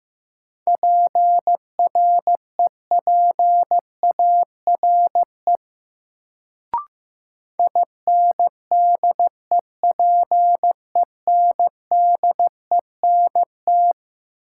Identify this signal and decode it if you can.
Morse code, heard as a keyed tone, about 15 wpm, 700 Hz. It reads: PREPARE INDEPENDENT
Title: Sentences from Top 900 Words - New Word Drill - 15wpm